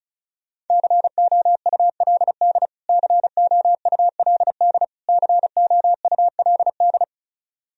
Morse code, heard as a keyed tone, about 36 wpm, 700 Hz. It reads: COULD COULD COULD
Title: Top 100 Words Repeated 35wpm